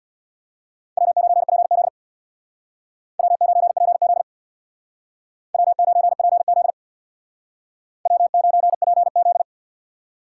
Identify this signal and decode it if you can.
Morse code, heard as a keyed tone, about 62 wpm, 700 Hz. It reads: W9PZ W9PZ W9PZ W9PZ
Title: Speed Racing - USA 1x2 Call Signs 50wpm